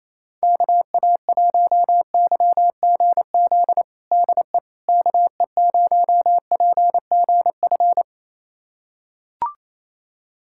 Morse code, heard as a keyed tone, about 28 wpm, 700 Hz. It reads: KA1YGZ DE KE0PGF
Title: USA Callsign DE Callsign - 6 Characters - 28wpm